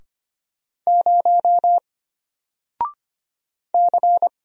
Morse code, heard as a keyed tone, about 25 wpm, 700 Hz. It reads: 0 C
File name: Single-Letter-Number-25wpm